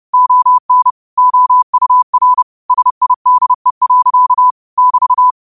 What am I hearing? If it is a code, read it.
Morse code, heard as a keyed tone, about 30 wpm, 1000 Hz. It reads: ON OUR SIDE. <BT>